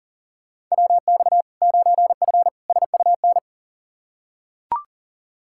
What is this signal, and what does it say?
Morse code, heard as a keyed tone, about 40 wpm, 700 Hz. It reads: WX 9F SUN
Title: Practice QSO - Exchange 2 - Info - 40wpm